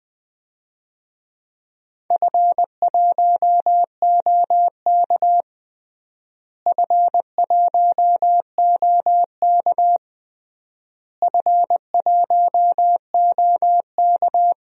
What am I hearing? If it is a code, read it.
Morse code, heard as a keyed tone, about 20 wpm, 700 Hz. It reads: F1OK F1OK F1OK
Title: DXCC-100 Call Signs - 4 Characters - Repeated 3x - 20wpm